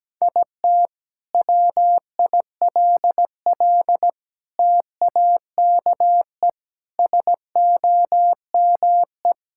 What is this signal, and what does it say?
Morse code, heard as a keyed tone, about 17 wpm, 700 Hz. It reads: IT WILL TAKE SOME
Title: Sentences from Top 1000 Words - Review Part 5 - 17wpm